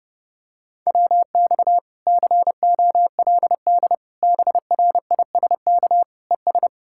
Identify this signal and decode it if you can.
Morse code, heard as a keyed tone, about 30 wpm, 700 Hz. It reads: WX COLD BRISK ES